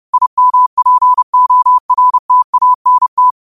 Morse code, heard as a keyed tone, about 30 wpm, 1000 Hz. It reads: IMPORTANT
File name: qotdfiles / 30wpm